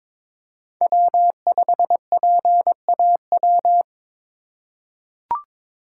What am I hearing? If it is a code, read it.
Morse code, heard as a keyed tone, about 21 wpm, 700 Hz. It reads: W5PAW
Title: USA Call Signs - 5 Characters - 3x Character Spacing - 22wpm